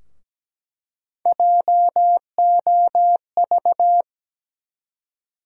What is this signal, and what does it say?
Morse code, heard as a keyed tone, about 17 wpm, 700 Hz. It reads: JOV
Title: USA Call Sign Postfixes - 3 Letters 17wpm